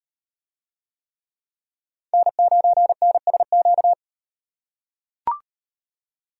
Morse code, heard as a keyed tone, about 38 wpm, 700 Hz. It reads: N9NSQ